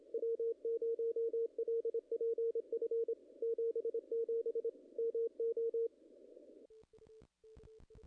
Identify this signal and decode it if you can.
Morse code, heard as a keyed tone, about 26 wpm, 440 Hz. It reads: W0LPF 77 MO
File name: contests / ss-cw-k5tr-op-n5rz -0000